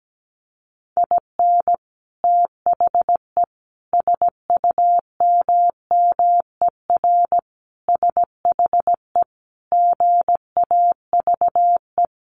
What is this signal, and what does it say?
Morse code, heard as a keyed tone, about 17 wpm, 700 Hz. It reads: IN THE SUMMER SHE GAVE
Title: Sentences from Top 1100 Words 17wpm